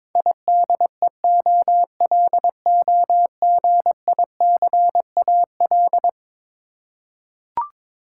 Morse code, heard as a keyed tone, about 22 wpm, 700 Hz. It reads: IDEOLOGICAL
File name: practice / 22wpm / Eleven Letter Words - 22wpm